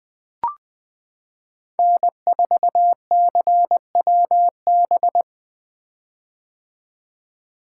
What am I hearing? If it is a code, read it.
Morse code, heard as a keyed tone, about 20 wpm, 700 Hz. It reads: N4CWB
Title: 2023 POTA Activator Call Signs - 20wpm